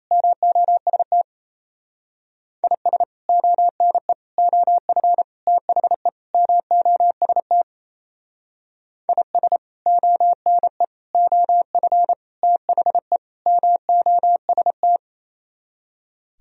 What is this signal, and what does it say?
Morse code, heard as a keyed tone, about 36 wpm, 700 Hz. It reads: MOST IS ONE OF THE MOST IS ONE OF THE MOST